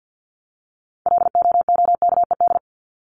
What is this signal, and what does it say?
Morse code, heard as a keyed tone, about 50 wpm, 700 Hz. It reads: LOOKED
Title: ED Suffix - Top 100 Words with Base Words 50wpm